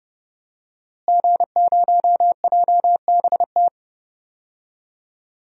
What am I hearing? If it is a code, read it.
Morse code, heard as a keyed tone, about 30 wpm, 700 Hz. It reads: G0JBT
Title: DXCC-100 Call Signs - 5 Characters - Repeated 3x - 30wpm